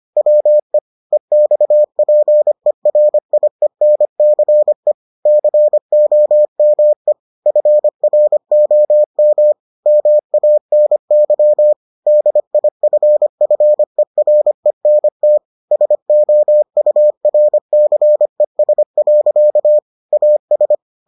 Morse code, heard as a keyed tone, about 25 wpm, 600 Hz. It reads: WE EXPERIENCE COME FROM MANY DIFFERENT SOURCES. AS